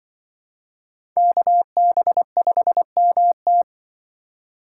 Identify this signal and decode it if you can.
Morse code, heard as a keyed tone, about 24 wpm, 700 Hz. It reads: KB5MT
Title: Speed Racing - USA 2x2 Call Signs 20wpm